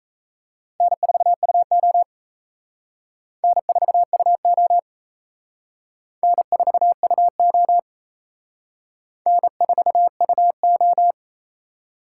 Morse code, heard as a keyed tone, about 34 wpm, 700 Hz. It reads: N4UO N4UO N4UO N4UO
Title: Speed Racing - USA 1x2 Call Signs 28wpm